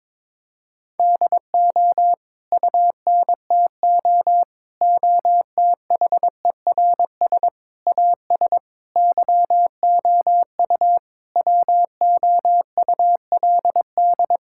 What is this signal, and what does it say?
Morse code, heard as a keyed tone, about 22 wpm, 700 Hz. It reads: DO UNTO OTHERS AS YOU WOULD
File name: Sets of 9 Words - 22wpm